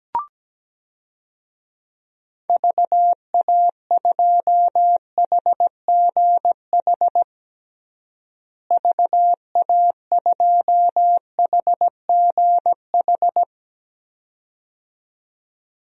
Morse code, heard as a keyed tone, about 17 wpm, 700 Hz. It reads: VA2HGH VA2HGH
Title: DXCC-100 Call Signs - 6 Characters - Repeated 2x - 17wpm